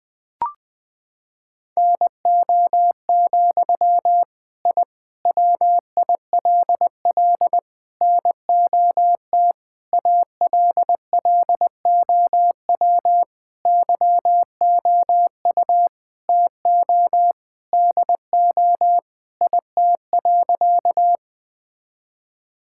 Morse code, heard as a keyed tone, about 20 wpm, 700 Hz. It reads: NO, I WILL NOT ALLOW YOU TO DO IT.
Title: Sentences from Top 1000 Words - Review Part 5 - 20wpm